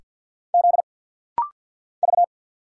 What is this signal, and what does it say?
Morse code, heard as a keyed tone, about 50 wpm, 690 Hz. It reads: Z V
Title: Single Letter-Number Alt - Rapid-Fire 50wpm